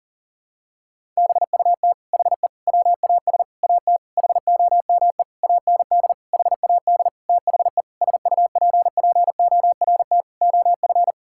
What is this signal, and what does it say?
Morse code, heard as a keyed tone, about 40 wpm, 700 Hz. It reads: BUT HE WAS AT HOME AND HAD THE SUPPORT OF
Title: Sentences from Top 1000 Words - Review Part 5 - 40wpm